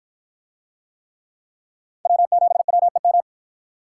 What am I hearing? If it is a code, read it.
Morse code, heard as a keyed tone, about 53 wpm, 700 Hz. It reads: W7WEM